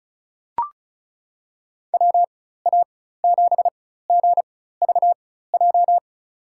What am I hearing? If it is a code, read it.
Morse code, heard as a keyed tone, about 34 wpm, spaced as for 16 wpm, 700 Hz. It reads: WA7GVJ
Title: USA Call Signs - 6 Characters - 4x Character Spacing - 35wpm